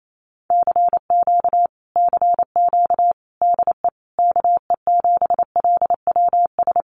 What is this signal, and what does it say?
Morse code, heard as a keyed tone, about 28 wpm, 700 Hz. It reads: CQ CQ DE KE7LWS
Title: Common QSO Phrases Encore 28wpm